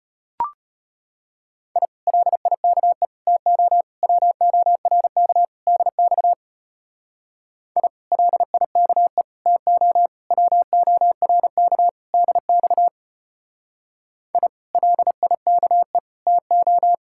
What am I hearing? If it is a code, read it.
Morse code, heard as a keyed tone, about 32 wpm, 700 Hz. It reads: I LIKE TO WORK DX I LIKE TO WORK DX I LIKE TO